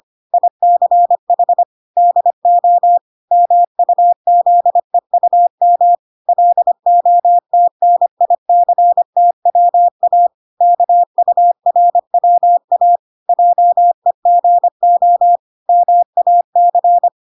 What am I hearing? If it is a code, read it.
Morse code, heard as a keyed tone, about 25 wpm, 700 Hz. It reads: ICH DO MUZEUM LOTNICTWA KURWA JEGO MAC